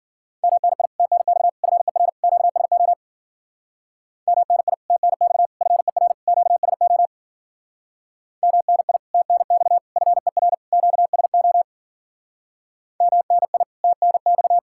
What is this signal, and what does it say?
Morse code, heard as a keyed tone, about 52 wpm, 700 Hz. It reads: MNI TNX FER QSO MNI TNX FER QSO MNI TNX FER QSO MNI TNX